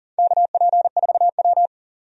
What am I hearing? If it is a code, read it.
Morse code, heard as a keyed tone, about 40 wpm, 700 Hz. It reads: KP4W
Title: DXCC-100 Call Signs - 4 Characters - Repeated 2x - 40wpm